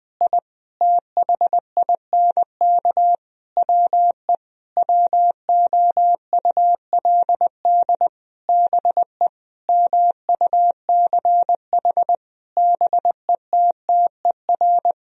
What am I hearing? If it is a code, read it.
Morse code, heard as a keyed tone, about 20 wpm, 700 Hz. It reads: I THINK WE WOULD BE MUCH BETTER
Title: Sentences from Top 1000 Words - Review Part 3 - 20wpm